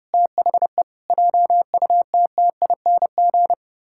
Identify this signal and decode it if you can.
Morse code, heard as a keyed tone, about 30 wpm, 700 Hz. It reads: THE JUTTING
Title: Pangram Challenge v2 - 30wpm